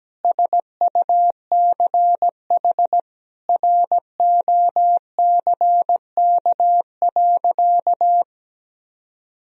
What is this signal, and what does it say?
Morse code, heard as a keyed tone, about 17 wpm, 700 Hz. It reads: SUCH ROCK.
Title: Sentences from Top 1000 Words - Review Part 2 - 17wpm